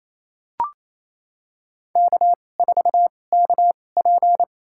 Morse code, heard as a keyed tone, about 28 wpm, spaced as for 20 wpm, 700 Hz. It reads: K4KP
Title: USA Call Signs - 4 Characters - 2x Character Spacing - 28wpm